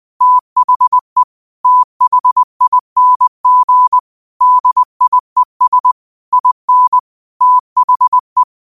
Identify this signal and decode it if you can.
Morse code, heard as a keyed tone, about 20 wpm, 1000 Hz. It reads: THE THING DIES IN THE
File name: qotdfiles / 20wpm